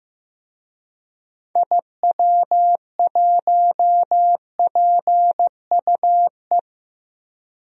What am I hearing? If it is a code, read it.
Morse code, heard as a keyed tone, about 15 wpm, 700 Hz. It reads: IW1PUE
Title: DXCC-100 Call Signs - 6 Characters - Repeated 3x - 15wpm